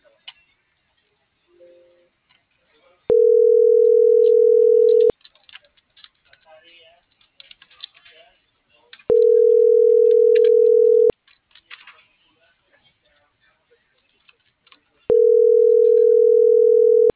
Ringback tone (North America), 440 + 480 Hz.